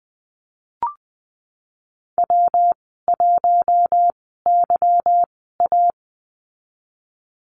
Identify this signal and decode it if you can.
Morse code, heard as a keyed tone, about 20 wpm, spaced as for 15 wpm, 700 Hz. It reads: W1YA
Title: 186 Lesson 30g - Letter b - Call Signs 20wpm